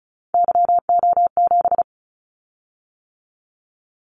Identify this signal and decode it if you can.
Morse code, heard as a keyed tone, about 36 wpm, 700 Hz. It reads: YO7
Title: Callsign Prefixes for Top 25 DXCC - 35wpm